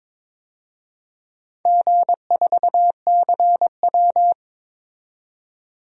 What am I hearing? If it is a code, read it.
Morse code, heard as a keyed tone, about 23 wpm, 700 Hz. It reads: G4CW
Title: DXCC-100 Call Signs - 4 Characters - 22wpm